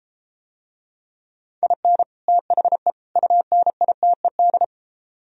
Morse code, heard as a keyed tone, about 33 wpm, 700 Hz. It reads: IN THE UNITED